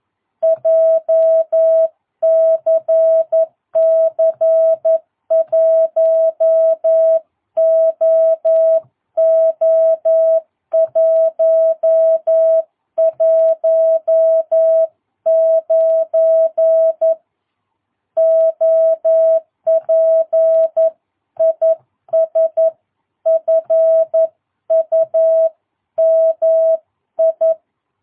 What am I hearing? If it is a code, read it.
Morse code, heard as a keyed tone, about 11 wpm, 660 Hz. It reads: JCC 1OO119 OP IS FUMI
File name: jh8cbh / 2022cw